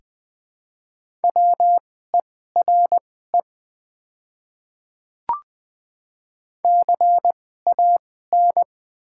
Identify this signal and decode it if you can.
Morse code, heard as a keyed tone, about 20 wpm, spaced as for 14 wpm, 700 Hz. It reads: WERE CAN
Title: 2x Character Spacing - Top 100 Words - 20wpm